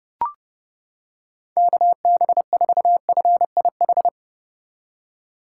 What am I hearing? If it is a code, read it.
Morse code, heard as a keyed tone, about 29 wpm, 700 Hz. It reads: KB4FIH